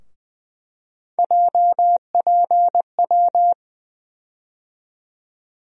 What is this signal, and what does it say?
Morse code, heard as a keyed tone, about 20 wpm, 700 Hz. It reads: JPW
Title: USA Call Sign Postfixes - 3 Letters 20wpm